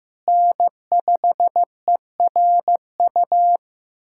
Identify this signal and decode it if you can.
Morse code, heard as a keyed tone, about 15 wpm, 700 Hz. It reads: N5ERU
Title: USA Callsign DE Callsign 15wpm